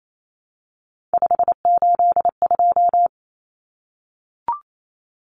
Morse code, heard as a keyed tone, about 28 wpm, 700 Hz. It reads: 582
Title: Numbers - 3-Digit 28wpm